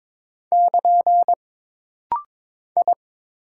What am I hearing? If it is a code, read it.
Morse code, heard as a keyed tone, about 22 wpm, 700 Hz. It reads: <KN> I
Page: This rapid-fire timing is geared to push your ICR (Instant Character Recognition) skills.